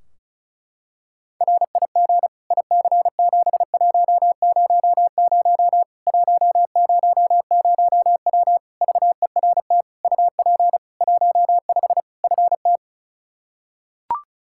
Morse code, heard as a keyed tone, about 34 wpm, 700 Hz. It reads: RIG IC7100 100W VERT UP 15 FT
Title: K1USN SST QSO Practice 35wpm